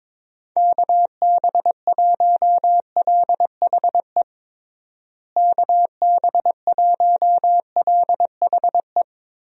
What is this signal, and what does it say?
Morse code, heard as a keyed tone, about 22 wpm, 700 Hz. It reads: KB1LHE KB1LHE
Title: USA Call Signs - 6 Characters - Repeated 2x - 22wpm